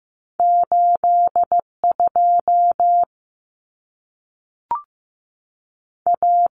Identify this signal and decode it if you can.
Morse code, heard as a keyed tone, about 15 wpm, 700 Hz. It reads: 82 A
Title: Common Cut Numbers - 2-Digit 15wpm